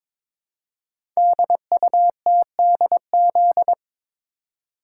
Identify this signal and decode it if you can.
Morse code, heard as a keyed tone, about 22 wpm, 700 Hz. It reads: DUTDZ
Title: Code Groups - 5 Letters - 22wpm